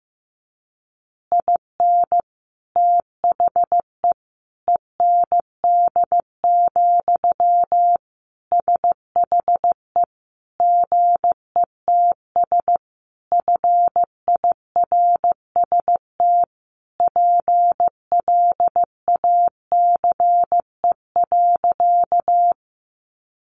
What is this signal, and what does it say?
Morse code, heard as a keyed tone, about 15 wpm, 700 Hz. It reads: IN THE END, SHE GETS FIRST PLACE.
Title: Additional Easy Phrases 15wpm